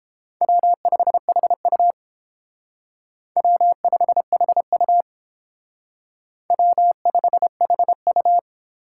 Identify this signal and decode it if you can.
Morse code, heard as a keyed tone, about 30 wpm, 700 Hz. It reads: W5HU W5HU W5HU